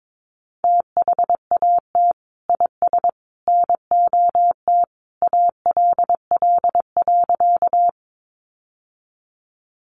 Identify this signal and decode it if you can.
Morse code, heard as a keyed tone, about 22 wpm, 700 Hz. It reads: THAT IS NOT ALL.
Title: Sentences from Top 600 Words 22wpm